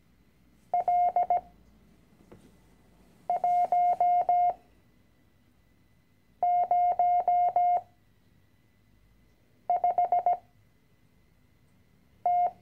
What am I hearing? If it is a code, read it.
Morse code, heard as a keyed tone, about 17 wpm, spaced as for 4 wpm, 710 Hz. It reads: L105T
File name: CW-Online-Kurs / Anfänger / Lektion 010